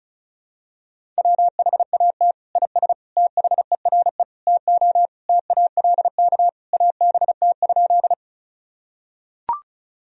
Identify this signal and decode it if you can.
Morse code, heard as a keyed tone, about 36 wpm, 700 Hz. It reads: WHAT IS THERE TO TALK ABT?
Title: Sentences from Top 400 Words and Abbreviations 35wpm